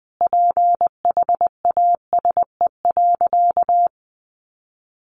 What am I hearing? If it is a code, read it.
Morse code, heard as a keyed tone, about 20 wpm, 700 Hz. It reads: PHASE.
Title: Sentences from Top 1200 Words 20wpm